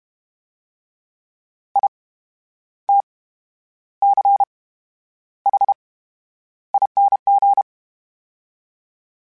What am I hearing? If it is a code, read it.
Morse code, heard as a keyed tone, about 32 wpm, spaced as for 11 wpm, 800 Hz. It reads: ITCHING
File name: Special INGs / Special INGs-mp3s / VerySlowToFaster Ing @30